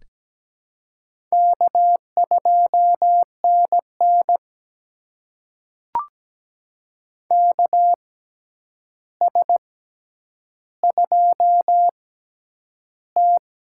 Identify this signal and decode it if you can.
Morse code, heard as a keyed tone, about 17 wpm, spaced as for 9 wpm, 700 Hz. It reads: K2NN KS2T
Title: USA Call Signs - 4 Characters - 6x Character Spacing - 17wpm